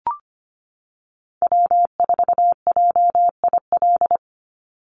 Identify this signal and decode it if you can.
Morse code, heard as a keyed tone, about 25 wpm, 700 Hz. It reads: W4JIL
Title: USA Call Signs 25wpm